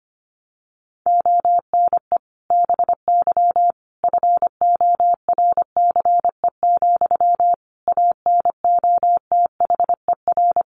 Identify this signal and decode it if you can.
Morse code, heard as a keyed tone, about 25 wpm, 700 Hz. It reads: ONE BY FORCE, ANOTHER
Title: Sentences from Top 1200 Words 25wpm